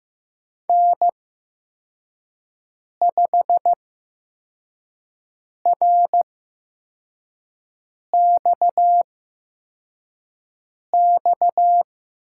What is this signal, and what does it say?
Morse code, heard as a keyed tone, about 15 wpm, spaced as for 4 wpm, 700 Hz. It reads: N5RXX
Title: USA Call Signs - 5 Characters - 8x Character Spacing - 15wpm